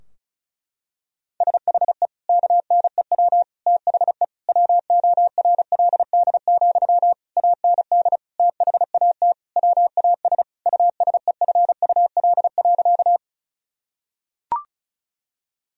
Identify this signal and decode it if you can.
Morse code, heard as a keyed tone, about 35 wpm, 700 Hz. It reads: SHE KNEW THE WORLD, AND THAT WAS USEFUL.
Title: Sentences from Top 1000 Words - Review Part 4 - 35wpm